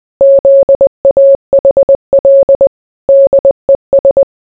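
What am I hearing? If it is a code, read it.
Morse code, heard as a keyed tone, about 20 wpm, 550 Hz. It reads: ZAHL DES
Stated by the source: Morse clip from a series, film, anime or game, posted for a guess-the-source game.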